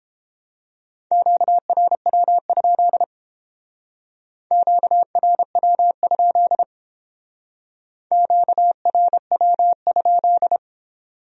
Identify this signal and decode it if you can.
Morse code, heard as a keyed tone, about 30 wpm, 700 Hz. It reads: QRW? QRW? QRW?